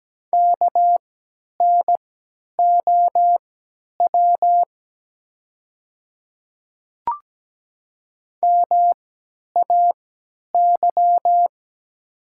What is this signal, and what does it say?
Morse code, heard as a keyed tone, about 17 wpm, spaced as for 10 wpm, 700 Hz. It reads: KNOW MAY
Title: 3x Character Spacing - Top 100 Words - 17wpm